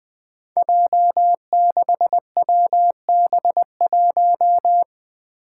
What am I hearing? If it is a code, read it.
Morse code, heard as a keyed tone, about 20 wpm, 700 Hz. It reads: J6WB1
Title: Five Challenging Letter-Number Combinations 20wpm